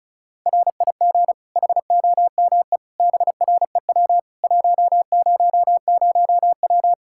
Morse code, heard as a keyed tone, about 34 wpm, 700 Hz. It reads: RIG HOME BREW 100W
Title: Practice QSO - Exchange 2 - Info - 35wpm